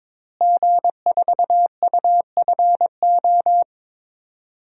Morse code, heard as a keyed tone, about 22 wpm, 700 Hz. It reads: G4UFO
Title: DXCC-100 Call Signs - 5 Characters - Repeated 2x - 22wpm